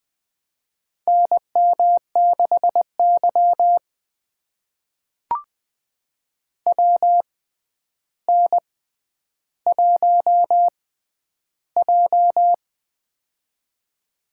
Morse code, heard as a keyed tone, about 20 wpm, spaced as for 10 wpm, 700 Hz. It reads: NM6Y WN1J
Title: USA Call Signs - 4 Characters - 6x Character Spacing - 20wpm